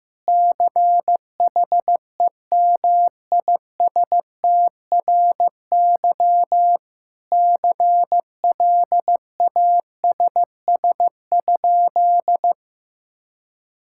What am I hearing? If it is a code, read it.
Morse code, heard as a keyed tone, about 15 wpm, 700 Hz. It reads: CHEMISTRY CLASS?
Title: Sentences from Top 1000 Words - Review Part 6 - 15wpm